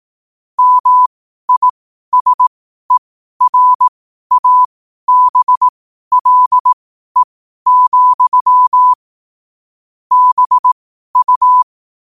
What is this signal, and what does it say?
Morse code, heard as a keyed tone, about 18 wpm, spaced as for 12 wpm, 1000 Hz. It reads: MISERABLE, BU
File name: qotdfiles / 12wpm